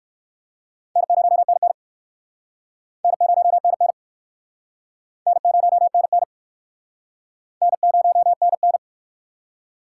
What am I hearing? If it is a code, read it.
Morse code, heard as a keyed tone, about 54 wpm, 700 Hz. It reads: N0NN N0NN N0NN N0NN
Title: Speed Racing - USA Call Signs 45wpm